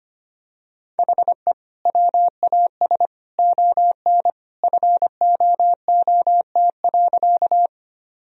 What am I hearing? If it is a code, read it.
Morse code, heard as a keyed tone, about 25 wpm, 700 Hz. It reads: HE WAS ON FOOT.
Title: Sentences from Top 1000 Words - Review Part 3 - 25wpm